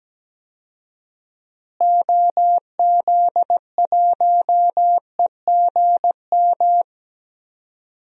Morse code, heard as a keyed tone, about 17 wpm, 700 Hz. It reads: OZ1EGM
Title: DXCC-100 Call Signs - 6 Characters - Repeated 3x - 17wpm